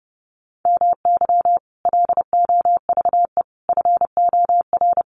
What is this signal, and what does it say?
Morse code, heard as a keyed tone, about 30 wpm, 700 Hz. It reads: MY LOVE FOR
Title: Sentences from Top 1400 Words 30wpm